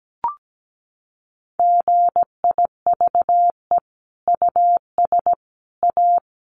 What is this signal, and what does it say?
Morse code, heard as a keyed tone, about 17 wpm, 700 Hz. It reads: GIVE US A
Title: Sentences from Top 1500 Words 17wpm